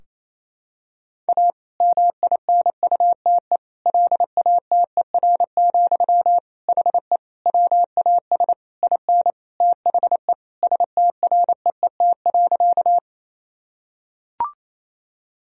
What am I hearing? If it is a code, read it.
Morse code, heard as a keyed tone, about 28 wpm, 700 Hz. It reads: A MINUTE LATER, HE WAS IN THE STREET.
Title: Sentences from Top 1000 Words - Review Part 4 - 28wpm